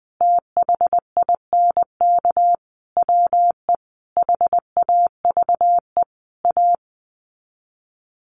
Morse code, heard as a keyed tone, about 19 wpm, 700 Hz. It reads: THINK WE HAVE A
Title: Sets of 5 Words Encore 20wpm